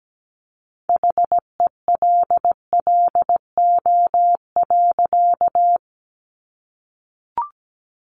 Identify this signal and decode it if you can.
Morse code, heard as a keyed tone, about 17 wpm, 700 Hz. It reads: HELLO.
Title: Sentences from Top 1100 Words 17wpm